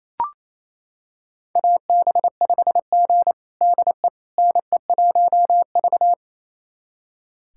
Morse code, heard as a keyed tone, about 28 wpm, 700 Hz. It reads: AB5G DE NE1V
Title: USA Callsign DE Callsign - 4 Characters - 28wpm